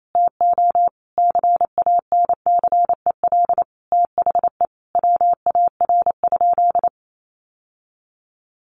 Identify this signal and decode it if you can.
Morse code, heard as a keyed tone, about 28 wpm, 700 Hz. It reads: TO CANCEL THE WAR?
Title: Sentences from Top 1200 Words 28wpm